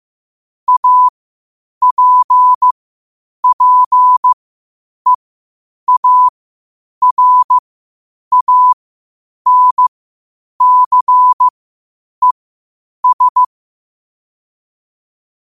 Morse code, heard as a keyed tone, about 15 wpm, spaced as for 8 wpm, 1000 Hz. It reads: APPEARANCES
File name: qotdfiles / 8wpm